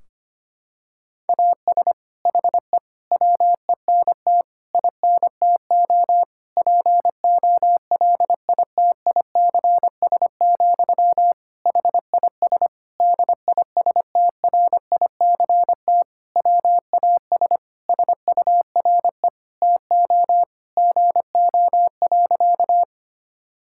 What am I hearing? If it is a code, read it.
Morse code, heard as a keyed tone, about 25 wpm, 700 Hz. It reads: AS HE WENT INTO POLITICS, HIS DISTRICT WAS SURE TO GO.
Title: Sentences from Top 1600 Words 25wpm